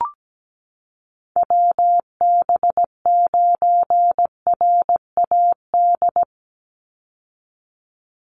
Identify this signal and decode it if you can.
Morse code, heard as a keyed tone, about 17 wpm, 700 Hz. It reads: WB9RAD
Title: USA Call Signs 17wpm